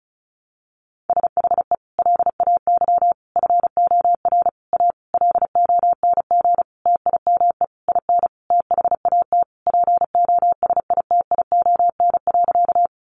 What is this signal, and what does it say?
Morse code, heard as a keyed tone, about 34 wpm, 700 Hz. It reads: SHE LAY FOR A LONG TIME IN THAT POSITION.
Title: Sentences from Top 900 Words 35wpm